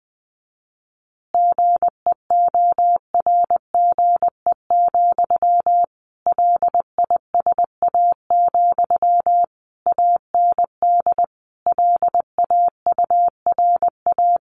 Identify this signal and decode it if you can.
Morse code, heard as a keyed tone, about 20 wpm, 700 Hz. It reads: GEORGE, LISA, AND LAURA
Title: Additional Easy Phrases 20wpm